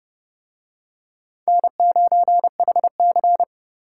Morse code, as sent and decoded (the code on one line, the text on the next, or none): -. ----. .... -.-.
N9HC